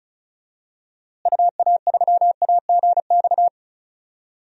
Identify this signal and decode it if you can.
Morse code, heard as a keyed tone, about 36 wpm, 700 Hz. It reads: UA3AGX
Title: DXCC-100 Call Signs - 6 Characters - Repeated 3x - 35wpm